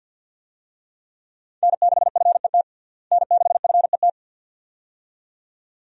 Morse code, heard as a keyed tone, about 52 wpm, 700 Hz. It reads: N6FET N6FET
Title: USA Call Signs - 5 Characters - Repeated 2x - 50wpm